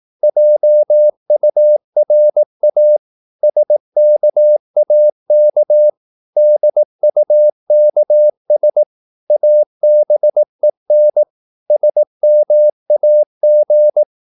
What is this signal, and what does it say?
Morse code, heard as a keyed tone, about 18 wpm, 600 Hz. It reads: JURA SKAK DUKS ABEN SMAG
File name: Korte ord DK 18wpm